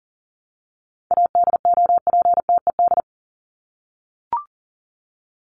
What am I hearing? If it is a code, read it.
Morse code, heard as a keyed tone, about 40 wpm, 700 Hz. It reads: ADOPTED
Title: ED Suffix - Top 100 Words with Base Words 40wpm